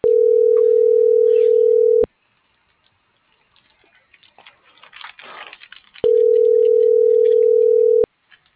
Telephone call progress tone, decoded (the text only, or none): ringback tone